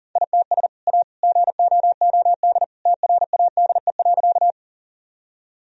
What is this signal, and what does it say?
Morse code, heard as a keyed tone, about 40 wpm, 700 Hz. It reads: ITS A GOOD TRADE.
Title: Sentences from Top 1000 Words - Review Part 4 - 40wpm